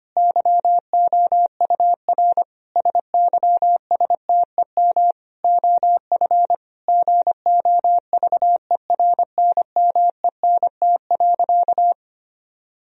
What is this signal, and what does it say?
Morse code, heard as a keyed tone, about 25 wpm, 700 Hz. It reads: YOUR SYSTEM OF GOVERNMENT.
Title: Sentences from Top 1000 Words - Review Part 3 - 25wpm